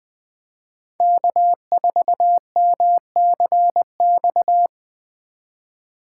Morse code, heard as a keyed tone, about 23 wpm, 700 Hz. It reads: K4MCX
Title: Speed Racing - USA 1x3 Call Signs 20wpm